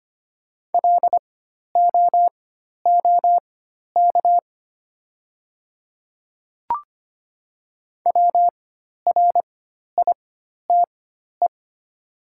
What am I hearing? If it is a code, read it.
Morse code, heard as a keyed tone, about 25 wpm, spaced as for 12 wpm, 700 Hz. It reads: LOOK WRITE
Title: 4x Character Spacing - Top 100 Words - 25wpm